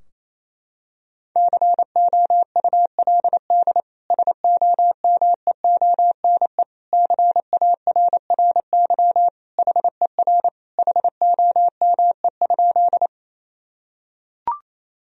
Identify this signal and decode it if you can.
Morse code, heard as a keyed tone, about 28 wpm, 700 Hz. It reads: COULD SOMEONE CARRY HER HOME?
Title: Sentences from Top 1000 Words - Review Part 4 - 28wpm